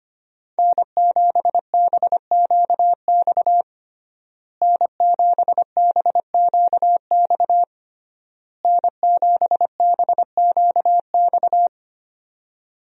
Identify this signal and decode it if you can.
Morse code, heard as a keyed tone, about 25 wpm, 700 Hz. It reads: N7BQX N7BQX N7BQX
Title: USA Call Signs - 5 Characters - Repeated 3x - 25wpm